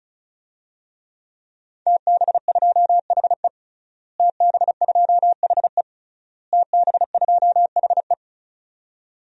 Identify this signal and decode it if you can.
Morse code, heard as a keyed tone, about 34 wpm, 700 Hz. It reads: TB2HE TB2HE TB2HE
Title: DXCC-100 Call Signs - 5 Characters - Repeated 3x - 35wpm